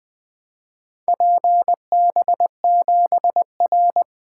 Morse code, heard as a keyed tone, about 20 wpm, 700 Hz. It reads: PB7R
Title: Four Letter-Number Combinations - 20wpm